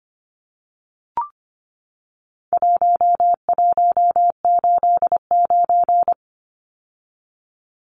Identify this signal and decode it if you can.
Morse code, heard as a keyed tone, about 25 wpm, 700 Hz. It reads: 1189
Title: Numbers - 4-Digit 25wpm